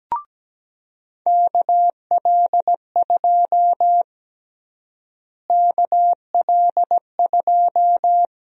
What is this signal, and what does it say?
Morse code, heard as a keyed tone, about 17 wpm, 700 Hz. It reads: KL2 KL2
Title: USA Call Sign Prefixes - Repeated 2x - 17wpm